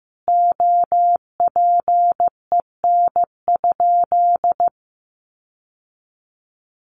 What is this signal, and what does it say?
Morse code, heard as a keyed tone, about 15 wpm, 700 Hz. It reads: OPEN?
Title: Sentences from Top 1300 Words 15wpm